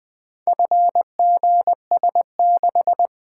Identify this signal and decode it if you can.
Morse code, heard as a keyed tone, about 20 wpm, 700 Hz. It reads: FGS6
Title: Four Letter-Number Combinations - 20wpm